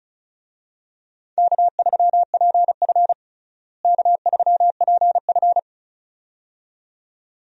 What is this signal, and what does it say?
Morse code, heard as a keyed tone, about 34 wpm, 700 Hz. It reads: K3PF K3PF
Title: USA Call Signs - 4 Characters - Repeated 2x - 35wpm